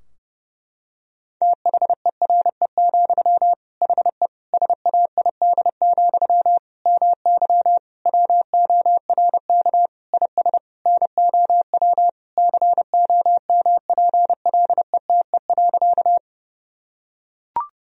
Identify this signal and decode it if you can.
Morse code, heard as a keyed tone, about 30 wpm, 700 Hz. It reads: THERE, HE SAID, MY WORK IS NOW COMPLETE.
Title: Sentences from Top 1000 Words - Review Part 2 - 30wpm